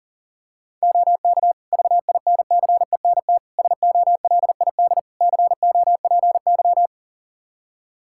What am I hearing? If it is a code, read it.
Morse code, heard as a keyed tone, about 40 wpm, 700 Hz. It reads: OK VINCENT SOLID COPY
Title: Practice QSO - Exchange 3 - More Info - 40wpm